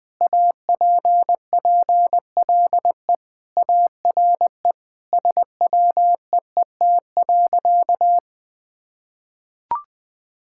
Morse code, heard as a keyed tone, about 20 wpm, 700 Hz. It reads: APPLE ARE SWEET.
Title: Sentences with 3-Word Phrases - Spoken Beginning - 20wpm